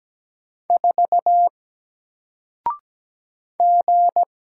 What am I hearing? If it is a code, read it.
Morse code, heard as a keyed tone, about 17 wpm, 700 Hz. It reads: 4 G
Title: Single Character (Letter:Number:Punctuation:Prosign) 17wpm